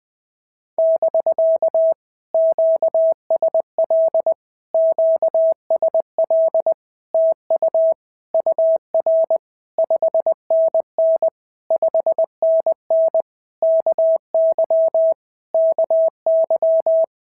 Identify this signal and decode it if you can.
Morse code, heard as a keyed tone, about 20 wpm, 650 Hz. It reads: <BK> QSL QSL TU UR 5NN 5NN KY KY
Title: POTA QSO Practice - 20wpm